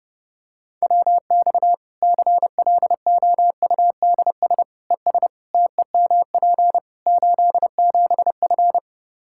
Morse code, heard as a keyed tone, about 30 wpm, 700 Hz. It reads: WX CLOUDS ES TEMP 87F
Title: Practice QSO - Exchange 2 - Info - 30wpm